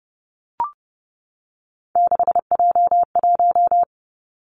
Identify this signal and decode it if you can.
Morse code, heard as a keyed tone, about 30 wpm, 700 Hz. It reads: 6J1
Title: Three Challenging Letter-Number Combinations 30wpm